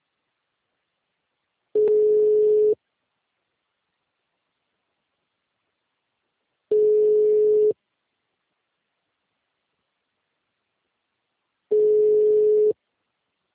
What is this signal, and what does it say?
Ringback tone (Europe, 425 Hz).
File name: out- -2006- - - .59